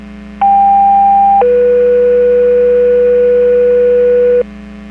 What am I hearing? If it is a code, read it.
Two-tone page, heard as a one-second tone, then a three-second tone, about 800 then 490 Hz, with noise. It is heard from 0.4 to 4.4 s.
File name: Portage 154.385 Paging